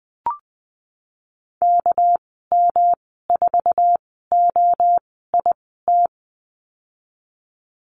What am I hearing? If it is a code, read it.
Morse code, heard as a keyed tone, about 19 wpm, spaced as for 14 wpm, 700 Hz. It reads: KM4OIT
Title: 238 Lesson 38g - Letter q - Call Signs 20wpm